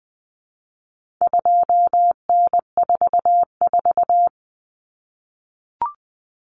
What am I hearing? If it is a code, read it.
Morse code, heard as a keyed tone, about 20 wpm, 700 Hz. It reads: 2N44
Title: Common Cut Numbers - 4-Digit 20wpm